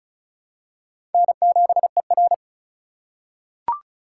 Morse code, heard as a keyed tone, about 36 wpm, 700 Hz. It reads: N7ER
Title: USA Call Signs - 4 Characters - 4x Character Spacing - 35wpm